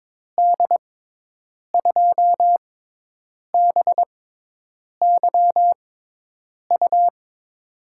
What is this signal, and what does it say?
Morse code, heard as a keyed tone, about 22 wpm, spaced as for 8 wpm, 700 Hz. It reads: D2BYU
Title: USA Call Signs - 6 Characters - 6x Character Spacing - 22wpm